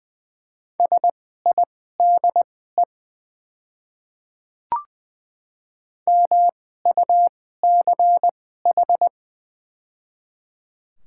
Morse code, heard as a keyed tone, about 20 wpm, spaced as for 14 wpm, 690 Hz. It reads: SIDE MUCH
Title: 2x Character Spacing - Top 200 Words - 20wpm